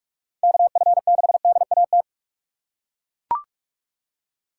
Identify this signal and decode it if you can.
Morse code, heard as a keyed tone, about 42 wpm, 700 Hz. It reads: KF6DAT
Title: USA Call Signs - 6 Characters - 6x Character Spacing - 45wpm